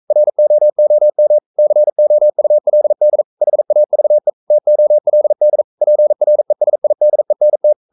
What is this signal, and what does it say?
Morse code, heard as a keyed tone, about 42 wpm, 600 Hz. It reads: ROOM COULD HAVE TOLD PRESIDENT
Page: Romanen Never af Ken Follet på engelsk.